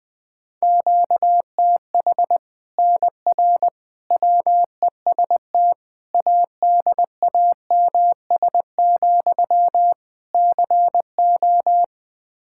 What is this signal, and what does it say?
Morse code, heard as a keyed tone, about 20 wpm, 700 Hz. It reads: QTH NR WEST ADAMS, CO
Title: Practice QSO - Exchange 1 - Beginning - 20wpm